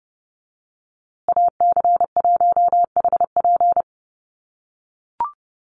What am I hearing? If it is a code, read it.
Morse code, heard as a keyed tone, about 30 wpm, 700 Hz. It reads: AC1HP
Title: USA Call Signs Repeated 2x 30wpm